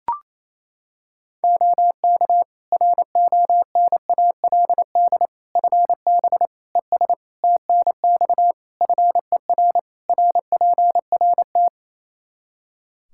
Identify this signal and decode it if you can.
Morse code, heard as a keyed tone, about 28 wpm, 700 Hz. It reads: OK RONALD FB ES TNX FER RPRT
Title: Practice QSO - Exchange 2 - Info - 28wpm